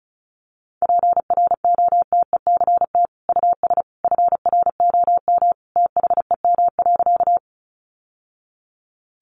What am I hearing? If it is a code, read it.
Morse code, heard as a keyed tone, about 34 wpm, 700 Hz. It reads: PROTECT US FROM THEM.
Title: Sentences from Top 900 Words 35wpm